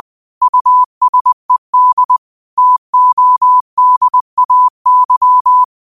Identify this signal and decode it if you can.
Morse code, heard as a keyed tone, about 20 wpm, 1000 Hz. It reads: USED TODAY
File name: qotdfiles / 20wpm